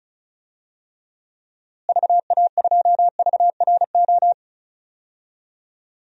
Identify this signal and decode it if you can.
Morse code, heard as a keyed tone, about 36 wpm, 700 Hz. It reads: VA2VRO
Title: DXCC-100 Call Signs - 6 Characters - Repeated 2x - 35wpm